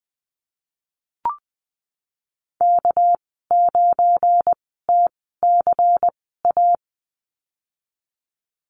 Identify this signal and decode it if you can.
Morse code, heard as a keyed tone, about 20 wpm, spaced as for 15 wpm, 700 Hz. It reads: K9TCA
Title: 219 Lesson 35g - Number 0 - Call Signs 20wpm